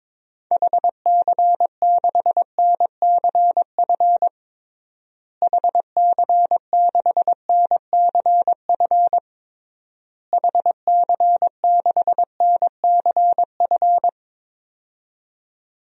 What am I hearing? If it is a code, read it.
Morse code, heard as a keyed tone, about 21 wpm, 700 Hz. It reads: HC6NCF HC6NCF HC6NCF
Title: DXCC-100 Call Signs - 6 Characters - Repeated 3x - 22wpm